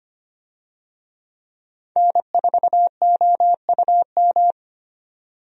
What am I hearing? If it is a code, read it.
Morse code, heard as a keyed tone, about 25 wpm, 700 Hz. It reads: N4OUM
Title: USA Call Signs - 5 Characters - 25wpm